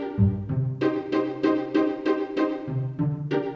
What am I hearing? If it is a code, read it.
There is no code.